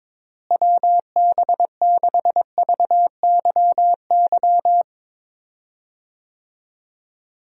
Morse code, heard as a keyed tone, about 21 wpm, 700 Hz. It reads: WB6VYY
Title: USA Call Signs - 6 Characters - Repeated 2x - 22wpm